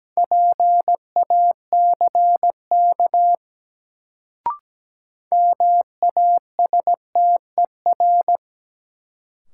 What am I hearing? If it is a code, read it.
Morse code, heard as a keyed tone, about 17 wpm, 700 Hz. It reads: PACK MASTER
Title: Sentences from Top 1300 Words - New Word Drill - 17wpm